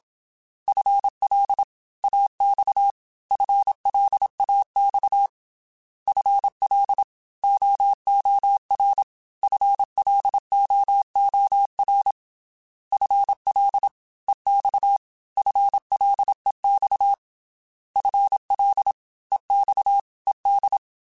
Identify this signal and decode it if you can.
Morse code, heard as a keyed tone, about 27 wpm, 800 Hz. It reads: FL AX FLAX FL OOR FLOOR FL EX FLEX FL EX ED
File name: FL / FL mp3s / Fast FL @25